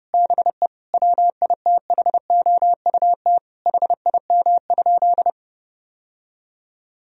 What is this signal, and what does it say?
Morse code, heard as a keyed tone, about 30 wpm, 700 Hz. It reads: BE WITHOUT HIM?
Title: Sentences from Top 1000 Words - Review Part 3 - 30wpm